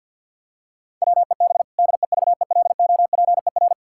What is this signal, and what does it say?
Morse code, heard as a keyed tone, about 46 wpm, 700 Hz. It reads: WEB DEVELOPER
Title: Practice QSO - Exchange 3 - More Info - 50wpm